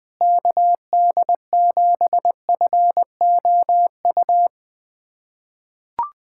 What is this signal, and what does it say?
Morse code, heard as a keyed tone, about 20 wpm, 700 Hz. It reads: KD7FOU
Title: USA Callsign DE Callsign 20wpm